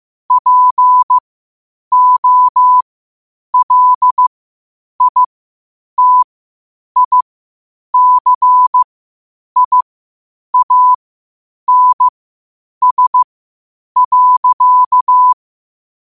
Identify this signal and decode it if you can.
Morse code, heard as a keyed tone, about 15 wpm, spaced as for 8 wpm, 1000 Hz. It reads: POLITICIANS.
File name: qotdfiles / 8wpm